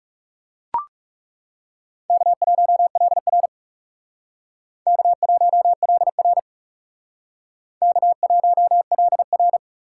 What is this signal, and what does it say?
Morse code, heard as a keyed tone, about 40 wpm, 700 Hz. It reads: K1LR K1LR K1LR